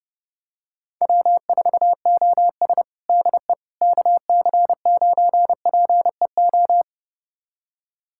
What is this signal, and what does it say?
Morse code, heard as a keyed tone, about 30 wpm, 700 Hz. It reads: W4OS DE KC9PEO
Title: USA Callsign DE Callsign 30wpm